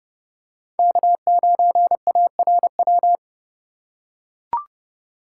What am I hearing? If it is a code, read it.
Morse code, heard as a keyed tone, about 31 wpm, 700 Hz. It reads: K9ARW
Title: USA Call Signs - 4x Character Spacing - 30wpm